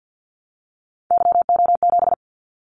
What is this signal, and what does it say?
Morse code, heard as a keyed tone, about 50 wpm, 700 Hz. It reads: YO7
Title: Callsign Prefixes for Top 25 DXCC - 50wpm